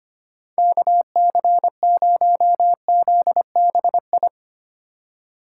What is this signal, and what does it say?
Morse code, heard as a keyed tone, about 31 wpm, 700 Hz. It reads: KC0ZBI